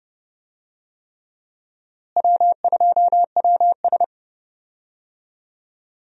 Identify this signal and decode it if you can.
Morse code, heard as a keyed tone, about 30 wpm, 700 Hz. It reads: W2WS
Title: USA Call Signs - 4 Characters - Repeated 3x - 30wpm